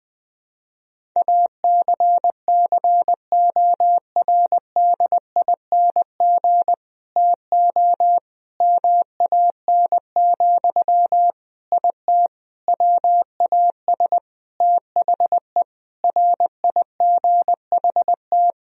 Morse code, heard as a keyed tone, about 20 wpm, 700 Hz. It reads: ACCORDING TO MAN, IT WAS THE RIGHT
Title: Sentences from Top 1000 Words - Review Part 4 - 20wpm